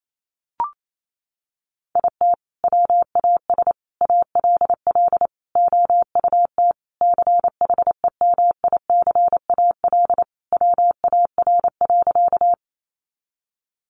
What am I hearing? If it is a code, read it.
Morse code, heard as a keyed tone, about 28 wpm, 700 Hz. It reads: IT WAS ALL OUT CHEMICAL WAR.
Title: Sentences from Top 1200 Words 28wpm